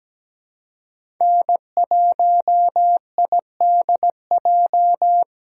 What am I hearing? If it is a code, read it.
Morse code, heard as a keyed tone, about 17 wpm, 700 Hz. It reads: N1IDJ
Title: USA Call Signs - 6x Character Spacing - 17wpm